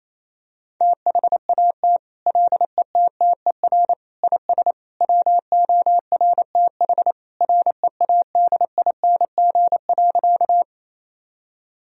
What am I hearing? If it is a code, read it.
Morse code, heard as a keyed tone, about 27 wpm, 700 Hz. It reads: THAT LETTER IS WORTH READING.
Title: Sentences from Top 1000 Words - Review Part 3 - 28wpm